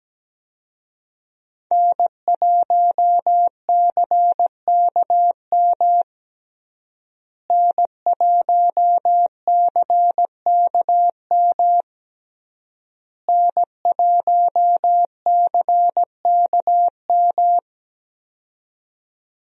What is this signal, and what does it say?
Morse code, heard as a keyed tone, about 17 wpm, 700 Hz. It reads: N1CKM N1CKM N1CKM
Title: USA Call Signs - 5 Characters - Repeated 3x - 17wpm